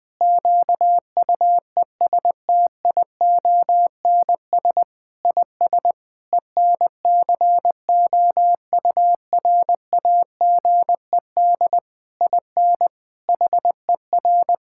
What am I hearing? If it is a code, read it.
Morse code, heard as a keyed tone, about 20 wpm, 700 Hz. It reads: QUESTIONS IS ENCOURAGED IN HER
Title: Difficult Sentences with Hints - 20wpm